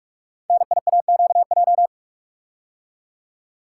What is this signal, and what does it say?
Morse code, heard as a keyed tone, about 44 wpm, 700 Hz. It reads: NIAQJ
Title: Code Groups - 5 Letters - 45wpm